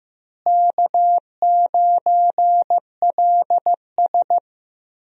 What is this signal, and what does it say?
Morse code, heard as a keyed tone, about 15 wpm, 700 Hz. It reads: K9LS
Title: USA Call Signs - 4 Characters - Repeated 2x - 15wpm